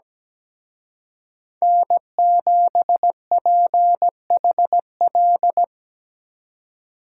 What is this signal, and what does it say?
Morse code, heard as a keyed tone, about 20 wpm, 700 Hz. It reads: N7PHL
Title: Speed Racing - USA 1x3 Call Signs 17wpm